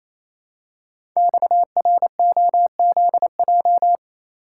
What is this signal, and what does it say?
Morse code, heard as a keyed tone, about 29 wpm, 700 Hz. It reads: XROZJ
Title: Code Groups - 5 Letters - 28wpm